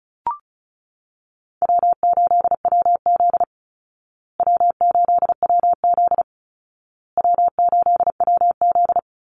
Morse code, heard as a keyed tone, about 34 wpm, 700 Hz. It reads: W8WZ W8WZ W8WZ
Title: Easy Call Signs Repeated 3x - 35wpm